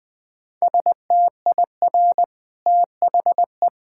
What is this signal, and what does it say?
Morse code, heard as a keyed tone, about 20 wpm, 700 Hz. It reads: STIR THE
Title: Easy Sentences with Hints - 20wpm